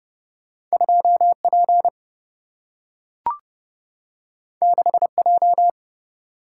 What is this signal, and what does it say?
Morse code, heard as a keyed tone, about 30 wpm, 700 Hz. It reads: 2 P 6 J
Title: Single-Letter Single-Number 30wpm